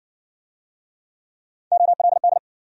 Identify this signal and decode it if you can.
Morse code, heard as a keyed tone, about 59 wpm, 700 Hz. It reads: OLD